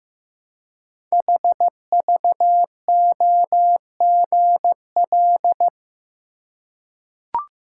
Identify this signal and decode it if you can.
Morse code, heard as a keyed tone, about 15 wpm, 700 Hz. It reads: HVOGL
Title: Code Groups - 5 Letter:Numbers - 15wpm